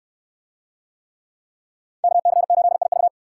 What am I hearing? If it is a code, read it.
Morse code, heard as a keyed tone, about 67 wpm, 700 Hz. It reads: KC8IV